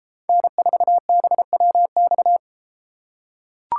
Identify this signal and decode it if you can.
Morse code, heard as a keyed tone, about 32 wpm, 700 Hz. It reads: N4BWX